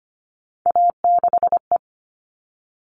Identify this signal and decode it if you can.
Morse code, heard as a keyed tone, about 25 wpm, 700 Hz. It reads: A6E
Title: All Cut Numbers - 3-Digit 25wpm